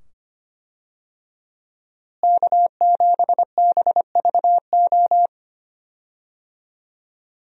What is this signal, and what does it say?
Morse code, heard as a keyed tone, about 25 wpm, 700 Hz. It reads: K7BVO
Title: USA Call Signs - 5 Characters - Repeated 2x - 25wpm